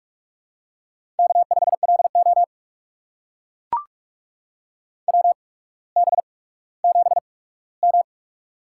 Morse code, heard as a keyed tone, about 44 wpm, spaced as for 19 wpm, 690 Hz. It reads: K5LO WB7M
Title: USA Call Signs - 4 Characters - 8x Character Spacing - 45wpm